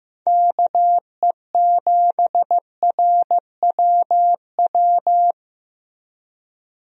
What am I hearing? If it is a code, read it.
Morse code, heard as a keyed tone, about 15 wpm, 700 Hz. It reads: KE7RWW
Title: USA Call Signs - 6 Characters - Repeated 2x - 15wpm